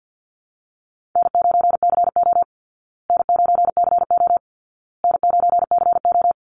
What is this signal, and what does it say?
Morse code, heard as a keyed tone, about 50 wpm, 700 Hz. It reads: N9CO N9CO N9CO
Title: Easy Call Signs Repeated 3x - 50wpm